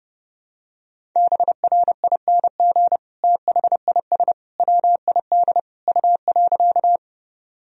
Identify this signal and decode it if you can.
Morse code, heard as a keyed tone, about 32 wpm, 700 Hz. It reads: BRING THIS WID U.
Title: Sentences from Top 500 Words and Abbreviations 30wpm